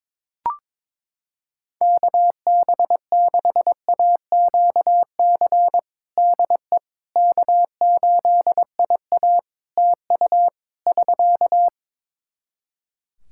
Morse code, heard as a keyed tone, about 22 wpm, 700 Hz. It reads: KB6AQC DE K8IA TU <SK>
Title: K1USN SST QSO Practice 22wpm